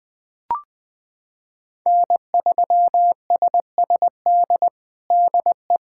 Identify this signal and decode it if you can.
Morse code, heard as a keyed tone, about 20 wpm, 700 Hz. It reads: N3SSD DE
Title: USA Callsign DE Callsign - 5 Characters - 20wpm